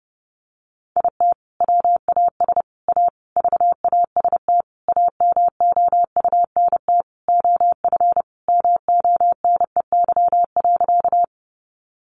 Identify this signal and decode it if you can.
Morse code, heard as a keyed tone, about 30 wpm, 700 Hz. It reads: IT WAS A VAST AMOUNT OF MONEY.
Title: Sentences from Top 1300 Words 30wpm